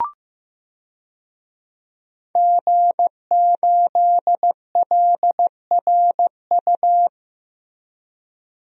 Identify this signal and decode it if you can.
Morse code, heard as a keyed tone, about 16 wpm, 700 Hz. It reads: G8LRU
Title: DXCC-100 Call Signs - 5 Characters - Repeated 3x - 15wpm